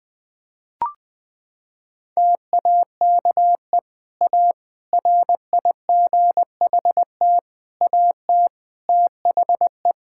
Morse code, heard as a keyed tone, about 21 wpm, 700 Hz. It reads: TAKE A RIGHT AT THE
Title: Sentences from Top 1600 Words 20wpm